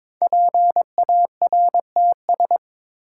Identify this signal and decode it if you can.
Morse code, heard as a keyed tone, about 21 wpm, 700 Hz. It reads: PARTS
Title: Sets of 3 Words - New Word Drill - 22wpm